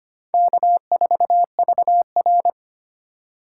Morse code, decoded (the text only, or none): K4VR